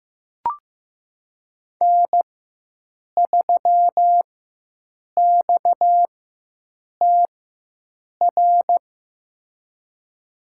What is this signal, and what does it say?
Morse code, heard as a keyed tone, about 15 wpm, spaced as for 7 wpm, 700 Hz. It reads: N3XTR
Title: USA Call Signs - 4x Character Spacing - 15wpm